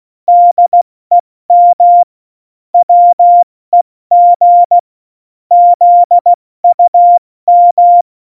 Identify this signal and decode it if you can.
Morse code, heard as a keyed tone, about 16 wpm, 700 Hz. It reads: DEM WEG ZUM
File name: Morsekurs / Witze / Witz 27. Oktober 2020, 70 F80